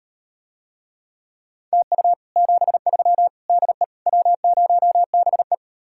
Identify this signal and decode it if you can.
Morse code, heard as a keyed tone, about 39 wpm, 700 Hz. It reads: TU 73 DE W0BE